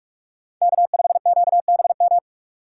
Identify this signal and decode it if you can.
Morse code, heard as a keyed tone, about 44 wpm, 700 Hz. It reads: K5QBM
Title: USA Call Signs - 5 Characters - 3x Character Spacing - 45wpm